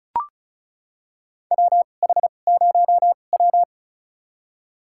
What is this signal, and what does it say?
Morse code, heard as a keyed tone, about 36 wpm, spaced as for 25 wpm, 700 Hz. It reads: WH0W
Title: USA Call Signs - 4 Characters - 2x Character Spacing - 35wpm